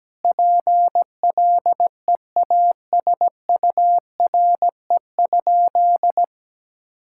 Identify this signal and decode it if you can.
Morse code, heard as a keyed tone, about 17 wpm, 700 Hz. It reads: PLEASURE?
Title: Sentences from Top 1600 Words 17wpm